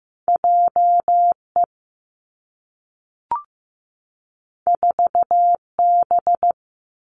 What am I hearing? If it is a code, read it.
Morse code, heard as a keyed tone, about 15 wpm, 700 Hz. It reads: J E 4 B
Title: Two Challenging Letter-Number Combinations 15wpm